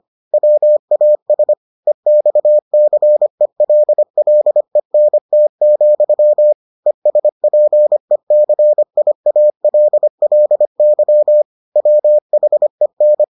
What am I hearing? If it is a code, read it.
Morse code, heard as a keyed tone, about 25 wpm, 600 Hz. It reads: WAS EXCELLENT, ESPECIALLY WHEN